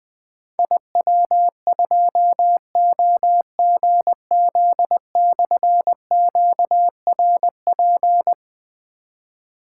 Morse code, heard as a keyed tone, about 20 wpm, 700 Hz. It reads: IW2OGZ/QRP
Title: Difficult Call Signs 20wpm